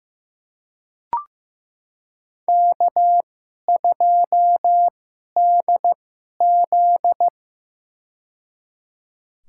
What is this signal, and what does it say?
Morse code, heard as a keyed tone, about 15 wpm, spaced as for 11 wpm, 700 Hz. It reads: K2DZ
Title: USA Call Signs - 4 Characters - 2x Character Spacing - 15wpm